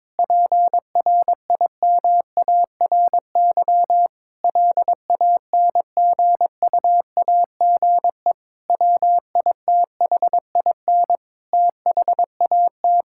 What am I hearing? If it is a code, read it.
Morse code, heard as a keyed tone, about 21 wpm, 700 Hz. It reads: PRIMARY LANGUAGE WITHIN THAT
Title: Sentences from Top 1000 Words - Review Part 5 - 22wpm